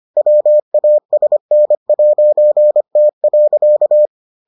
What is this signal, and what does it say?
Morse code, heard as a keyed tone, about 25 wpm, 600 Hz. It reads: WASN'T.